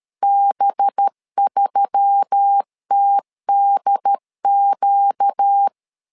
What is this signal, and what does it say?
Morse code, heard as a keyed tone, about 13 wpm, 800 Hz. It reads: B3TDQ
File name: LEZIONI / 5 LBVC / 5 LB 12wpm 1906